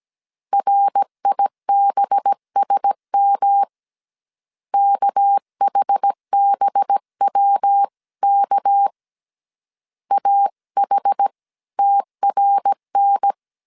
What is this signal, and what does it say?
Morse code, heard as a keyed tone, about 17 wpm, 800 Hz. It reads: RIBSM KHBWK AH TRN
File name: 16wpm